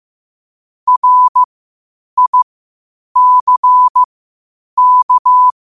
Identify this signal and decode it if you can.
Morse code, heard as a keyed tone, about 15 wpm, spaced as for 8 wpm, 1000 Hz. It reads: RICK
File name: qotdfiles / 8wpm